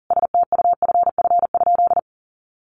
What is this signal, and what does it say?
Morse code, heard as a keyed tone, about 40 wpm, 700 Hz. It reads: STUFF?
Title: Sentences from Top 1400 Words 40wpm